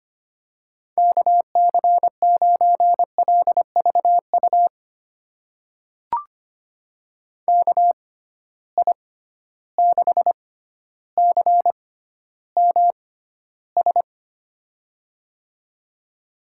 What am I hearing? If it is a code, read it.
Morse code, heard as a keyed tone, about 25 wpm, spaced as for 13 wpm, 700 Hz. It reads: KC9LVU KI6CMS
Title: USA Call Signs - 6 Characters - 6x Character Spacing - 25wpm